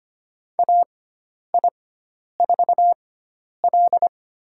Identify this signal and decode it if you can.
Morse code, heard as a keyed tone, about 25 wpm, spaced as for 10 wpm, 700 Hz. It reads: AI4L
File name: Lesson 16g - Letter c - Call Signs 25wpm